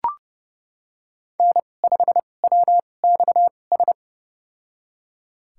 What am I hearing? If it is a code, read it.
Morse code, heard as a keyed tone, about 30 wpm, spaced as for 22 wpm, 700 Hz. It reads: N5WXS
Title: USA Call Signs - 5 Characters - 2x Character Spacing - 30wpm